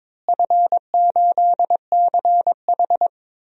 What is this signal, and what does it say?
Morse code, heard as a keyed tone, about 22 wpm, 700 Hz. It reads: F8CH
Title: DXCC-100 Call Signs - 4 Characters - Repeated 3x - 22wpm